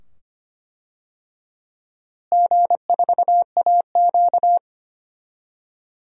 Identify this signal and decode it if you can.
Morse code, heard as a keyed tone, about 25 wpm, 700 Hz. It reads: G4AQ
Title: DXCC-100 Call Signs - 4 Characters - Repeated 2x - 25wpm